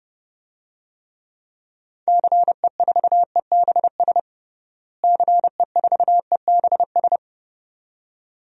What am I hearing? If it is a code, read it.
Morse code, heard as a keyed tone, about 32 wpm, 700 Hz. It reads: CE4EBS CE4EBS
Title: DXCC-100 Call Signs - 6 Characters - Repeated 2x - 30wpm